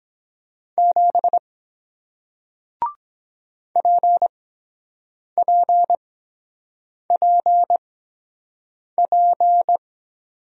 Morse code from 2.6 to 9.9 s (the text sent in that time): PPPP